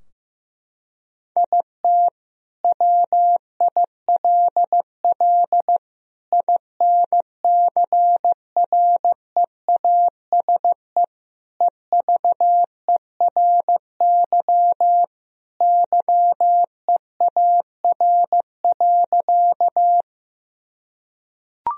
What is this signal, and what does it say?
Morse code, heard as a keyed tone, about 15 wpm, 700 Hz. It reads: IT WILL INCREASE EVERY YEAR.
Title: Sentences from Top 1000 Words - Review Part 4 - 15wpm